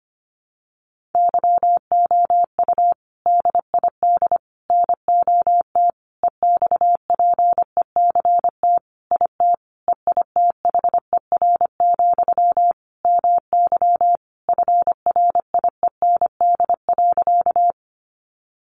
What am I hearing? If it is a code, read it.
Morse code, heard as a keyed tone, about 25 wpm, 700 Hz. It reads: YOU DID NOT EXPECT IT EITHER, MY FRIEND.
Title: Sentences from Top 1100 Words 25wpm